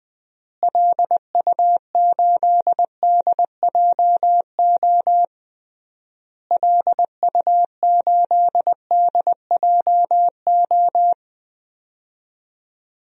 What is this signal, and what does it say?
Morse code, heard as a keyed tone, about 20 wpm, 700 Hz. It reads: LU8DJO LU8DJO
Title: DXCC-100 Call Signs - 6 Characters - Repeated 3x - 20wpm